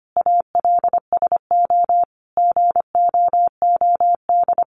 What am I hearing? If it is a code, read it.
Morse code, heard as a keyed tone, about 25 wpm, 700 Hz. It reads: ALSO GOOD
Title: Sentences from Top 1300 Words 25wpm